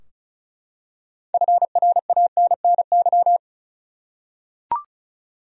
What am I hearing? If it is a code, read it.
Morse code, heard as a keyed tone, about 34 wpm, 700 Hz. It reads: FRANNY
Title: USA Common Names - 35wpm